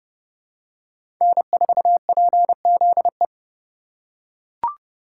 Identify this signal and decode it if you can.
Morse code, heard as a keyed tone, about 30 wpm, 700 Hz. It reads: N4PZE
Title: USA Call Signs - 5 Characters - 3x Character Spacing - 30wpm